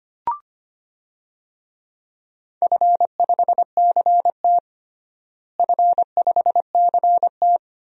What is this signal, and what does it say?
Morse code, heard as a keyed tone, about 25 wpm, 700 Hz. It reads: F5CT F5CT
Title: DXCC-100 Call Signs - 4 Characters - Repeated 2x - 25wpm